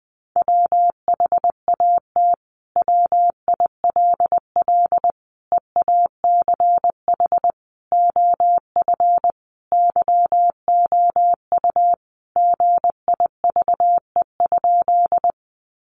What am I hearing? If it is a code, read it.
Morse code, heard as a keyed tone, about 20 wpm, 700 Hz. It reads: WHAT WILL EACH OF YOU GIVE?
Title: Sentences from Top 200 Words 20wpm